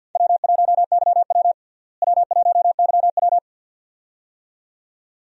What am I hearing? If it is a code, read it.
Morse code, heard as a keyed tone, about 50 wpm, 700 Hz. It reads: W1YW W1YW
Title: USA Call Signs - 4 Characters - Repeated 2x - 50wpm